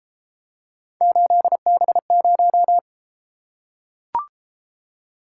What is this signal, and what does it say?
Morse code, heard as a keyed tone, about 32 wpm, 700 Hz. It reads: 8B0